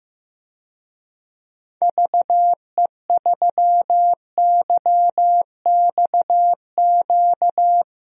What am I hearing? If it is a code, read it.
Morse code, heard as a keyed tone, about 15 wpm, 700 Hz. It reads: VE3YXQ
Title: DXCC-100 Call Signs - 6 Characters - Repeated 3x - 15wpm